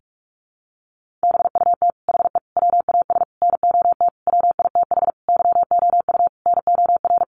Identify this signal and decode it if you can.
Morse code, heard as a keyed tone, about 44 wpm, 700 Hz. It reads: BUT HE WAS NOT WITH YOU NOR
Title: Sentences from Top 1500 Words 45wpm